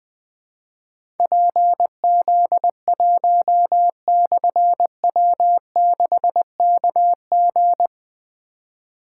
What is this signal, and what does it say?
Morse code, heard as a keyed tone, about 21 wpm, 700 Hz. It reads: PZ1/W6KG
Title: Difficult Call Signs 20wpm